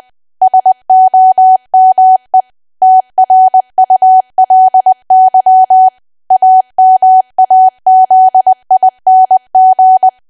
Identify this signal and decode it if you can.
Morse code, heard as a keyed tone, about 20 wpm, 750 Hz. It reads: SOME TRULY AMAZING
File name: cwa / W1AW-Level3 / 20WPM